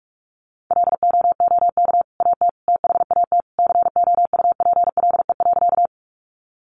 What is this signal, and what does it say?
Morse code, heard as a keyed tone, about 44 wpm, 700 Hz. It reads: LOOK AT THAT COUPLE.
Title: Sentences from Top 1100 Words 45wpm